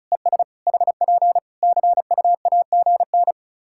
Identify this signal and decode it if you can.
Morse code, heard as a keyed tone, about 34 wpm, 700 Hz. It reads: ES HP CUAGN
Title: K1USN SST QSO Practice 35wpm